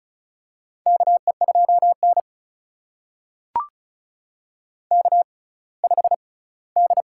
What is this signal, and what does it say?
Morse code, heard as a keyed tone, about 36 wpm, spaced as for 20 wpm, 700 Hz. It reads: KE2N K5D
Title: USA Call Signs - 4 Characters - 6x Character Spacing - 35wpm